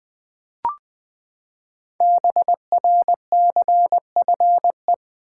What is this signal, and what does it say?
Morse code, heard as a keyed tone, about 20 wpm, 700 Hz. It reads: BRCFE
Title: Code Groups - 5 Letters - 20wpm